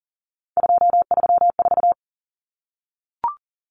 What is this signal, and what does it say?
Morse code, heard as a keyed tone, about 40 wpm, 700 Hz. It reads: 234
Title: Numbers - 3-Digit 40wpm